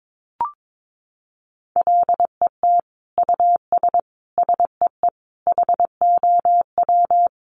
Morse code, heard as a keyed tone, about 22 wpm, 700 Hz. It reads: LET US SEE HOW
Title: Sentences from Top 1400 Words 22wpm